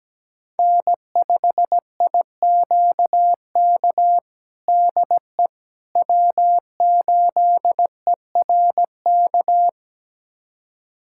Morse code, heard as a keyed tone, about 17 wpm, 700 Hz. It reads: N5IQK DE W8ERK
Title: USA Callsign DE Callsign - 5 Characters - 17wpm